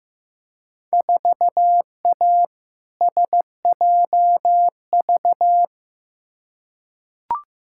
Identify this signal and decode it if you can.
Morse code, heard as a keyed tone, about 15 wpm, 700 Hz. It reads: 4A SJV
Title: Field Day Exchanges Encore - 15wpm